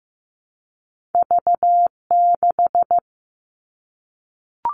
Morse code, heard as a keyed tone, about 15 wpm, 700 Hz. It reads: V6
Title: Two Challenging Letter-Number Combinations 15wpm